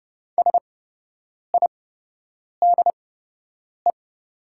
Morse code, heard as a keyed tone, about 30 wpm, spaced as for 8 wpm, 700 Hz. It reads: SIDE
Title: 8x Character Spacing - Top 300 Words - 30wpm